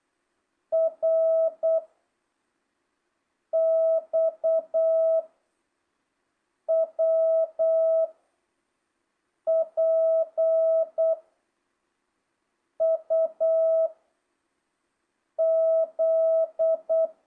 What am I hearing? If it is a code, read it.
Morse code, heard as a keyed tone, about 8 wpm, 660 Hz. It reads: R X W P U Z
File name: jh8cbh / 2025cw